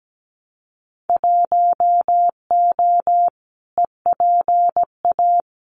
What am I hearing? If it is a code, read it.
Morse code, heard as a keyed tone, about 17 wpm, 700 Hz. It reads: 1O EPA
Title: Winter Field Day Exchanges 17wpm